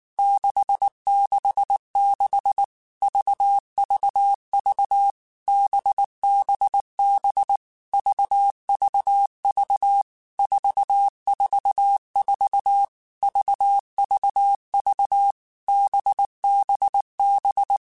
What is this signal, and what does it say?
Morse code, heard as a keyed tone, about 19 wpm, 800 Hz. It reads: 666 VVV BBB VVV 444 VVV BBB
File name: Basic Letter Groups By7Groups / VB46 / VB46@18